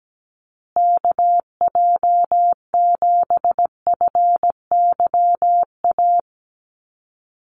Morse code, heard as a keyed tone, about 17 wpm, 700 Hz. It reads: KJ7FYA
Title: USA Call Signs Repeated 2x 17wpm